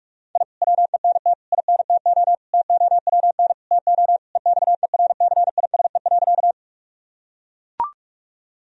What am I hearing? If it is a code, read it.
Morse code, heard as a keyed tone, about 44 wpm, 700 Hz. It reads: I WENT INTO TOWN TO EXERCISE.
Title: Sentences from Top 1000 Words - Review Part 4 - 45wpm